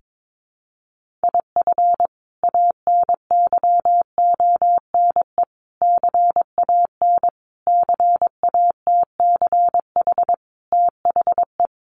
Morse code, heard as a keyed tone, about 22 wpm, 700 Hz. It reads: IF ANYONE CAN CATCH THE
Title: Sentences from Top 1500 Words 22wpm